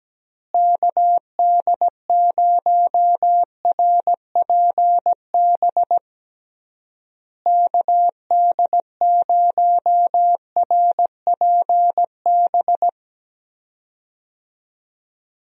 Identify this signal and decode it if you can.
Morse code, heard as a keyed tone, about 17 wpm, 700 Hz. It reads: KD0RPB KD0RPB
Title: USA Call Signs - 6 Characters - Repeated 3x - 17wpm